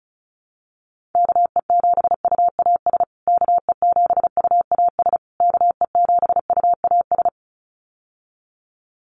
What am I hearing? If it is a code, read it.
Morse code, heard as a keyed tone, about 34 wpm, 700 Hz. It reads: KE7UAS KE7UAS KE7UAS
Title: USA Call Signs Repeated 3x 35wpm